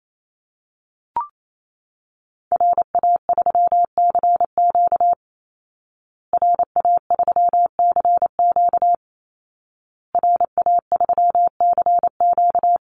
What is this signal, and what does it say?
Morse code, heard as a keyed tone, about 28 wpm, 700 Hz. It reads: RA3CQ RA3CQ RA3CQ
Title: Call Signs Repeated 2x - 28wpm